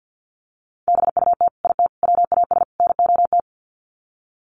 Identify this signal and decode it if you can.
Morse code, heard as a keyed tone, about 48 wpm, 700 Hz. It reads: BUT IT WAS NOT
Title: Sets of 4 Words 50wpm